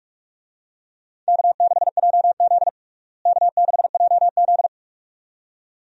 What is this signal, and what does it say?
Morse code, heard as a keyed tone, about 41 wpm, 700 Hz. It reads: K6JZ K6JZ
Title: USA Call Signs - 4 Characters - Repeated 2x - 45wpm